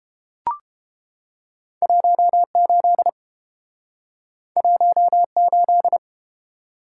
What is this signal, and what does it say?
Morse code, heard as a keyed tone, about 30 wpm, 700 Hz. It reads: 18 18